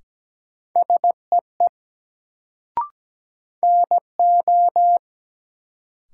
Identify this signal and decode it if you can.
Morse code, heard as a keyed tone, about 17 wpm, 700 Hz. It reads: SEE NO
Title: Sentences from Top 100 Words - New Word Drill - 17wpm